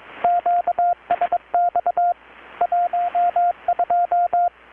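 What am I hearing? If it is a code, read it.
Morse code, heard as a keyed tone, about 21 wpm, 720 Hz, with radio noise. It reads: QSX 12
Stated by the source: Morse from KPH's Night of Nights XX maritime radiotelegraphy event, received in Chiapas, Mexico about 3,800 km away on a Kenwood R-600 with a grounded 10 m long-wire antenna.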